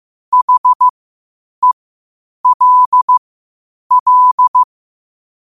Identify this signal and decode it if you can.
Morse code, heard as a keyed tone, about 15 wpm, spaced as for 8 wpm, 1000 Hz. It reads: HELL
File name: qotdfiles / 8wpm